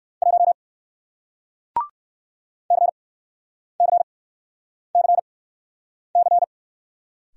Morse code, heard as a keyed tone, about 53 wpm, spaced as for 9 wpm, 700 Hz. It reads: 1 CCCC
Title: Speed-Racing Single Letter-Number 45wpm